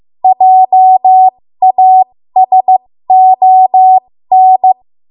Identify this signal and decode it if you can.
Morse code, heard as a keyed tone, about 15 wpm, spaced as for 13 wpm, 750 Hz. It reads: JASON
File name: Morse / 13WPM